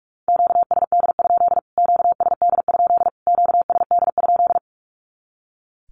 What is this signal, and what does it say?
Morse code, heard as a keyed tone, about 44 wpm, 700 Hz. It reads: QSD? QSD? QSD?
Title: Q-Codes Repeated 3x 45wpm